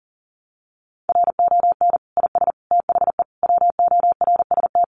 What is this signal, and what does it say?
Morse code, heard as a keyed tone, about 40 wpm, 700 Hz. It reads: RON IS THE WORST
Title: Additional Easy Phrases 40wpm